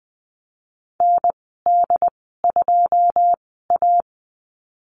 Morse code, heard as a keyed tone, about 19 wpm, spaced as for 14 wpm, 700 Hz. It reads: ND2A
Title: 186 Lesson 30g - Letter b - Call Signs 20wpm